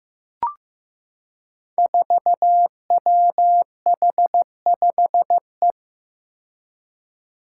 Morse code, heard as a keyed tone, about 15 wpm, 700 Hz. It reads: 4WH5E
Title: Five Challenging Letter-Number Combinations 15wpm